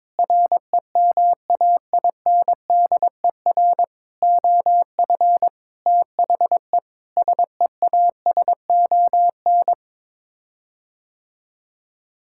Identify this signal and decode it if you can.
Morse code, heard as a keyed tone, about 22 wpm, 700 Hz. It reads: REMAINDER OF THE SEASON
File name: practice / 22wpm / Sets of 6 Words - 22wpm